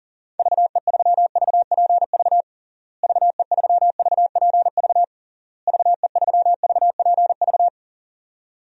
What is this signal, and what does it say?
Morse code, heard as a keyed tone, about 40 wpm, 700 Hz. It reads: VE3VPV VE3VPV VE3VPV
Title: DXCC-100 Call Signs - 6 Characters - Repeated 3x - 40wpm